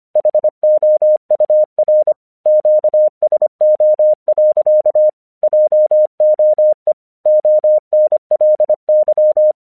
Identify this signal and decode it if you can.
Morse code, heard as a keyed tone, about 25 wpm, 610 Hz. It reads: HOUR QSO. JOE ONLY